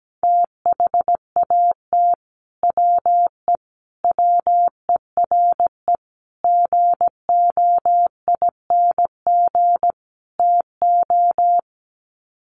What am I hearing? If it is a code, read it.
Morse code, heard as a keyed tone, about 17 wpm, 700 Hz. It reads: THAT WE WERE GOING TO
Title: Sets of 5 Words Encore2 17wpm